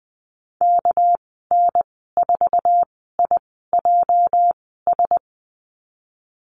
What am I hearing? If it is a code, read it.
Morse code, heard as a keyed tone, about 20 wpm, spaced as for 14 wpm, 700 Hz. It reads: KN4IJS
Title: 219 Lesson 35g - Number 0 - Call Signs 20wpm